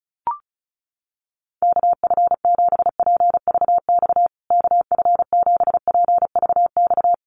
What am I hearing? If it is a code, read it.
Morse code, heard as a keyed tone, about 34 wpm, 700 Hz. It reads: KF7PVX KF7PVX
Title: USA Call Signs Repeated 3x 35wpm